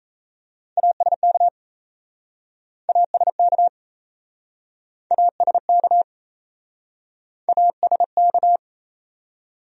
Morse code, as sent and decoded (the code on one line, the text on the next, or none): .- ... -.- / .- ... -.- / .- ... -.- / .- ... -.-
ASK ASK ASK ASK